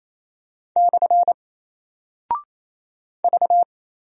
Morse code, heard as a keyed tone, about 28 wpm, 700 Hz. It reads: / V
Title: Single Character (Letter:Number:Punctuation:Prosign) Alt 28wpm